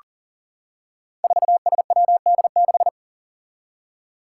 Morse code, heard as a keyed tone, about 39 wpm, 700 Hz. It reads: 4SWD6
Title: Five Challenging Letter-Number Combinations 40wpm